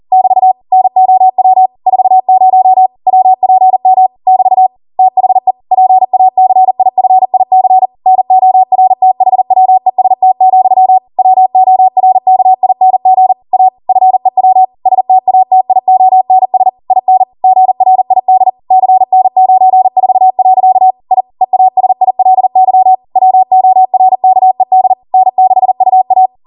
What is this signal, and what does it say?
Morse code, heard as a keyed tone, about 40 wpm, 750 Hz. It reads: <BT> NOW 40 WPM <BT> THE PACIFIC NORTHWEST, WORKING A FEW STATIONS IN GRID CN94. I EASILY WORKED N6UA